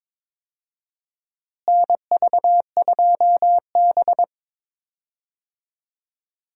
Morse code, heard as a keyed tone, about 22 wpm, 700 Hz. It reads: NV2B
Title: USA Call Signs - 4 Characters - Repeated 2x - 22wpm